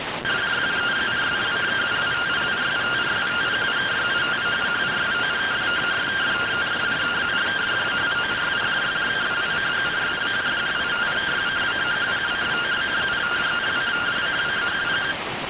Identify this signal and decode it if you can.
Two-tone teleprinter signal, about 170 Hz shift.